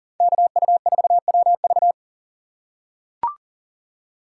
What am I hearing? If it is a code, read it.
Morse code, heard as a keyed tone, about 40 wpm, 700 Hz. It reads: KU4WV
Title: USA Call Signs - 5 Characters - 2x Character Spacing - 40wpm